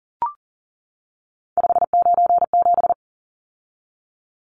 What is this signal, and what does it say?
Morse code, heard as a keyed tone, about 40 wpm, 700 Hz. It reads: 597
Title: Numbers - 3-Digit 40wpm